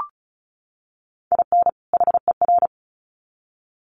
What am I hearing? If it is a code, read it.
Morse code, heard as a keyed tone, about 34 wpm, 700 Hz. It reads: IN HER
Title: Sets of 2 Words 35wpm